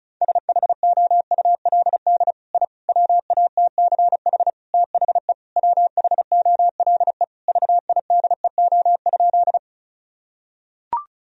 Morse code, heard as a keyed tone, about 36 wpm, 700 Hz. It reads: SHOULD I WATCH THE WHOLE VIDEO?
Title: Sentences from Top 1000 Words - Review Part 4 - 35wpm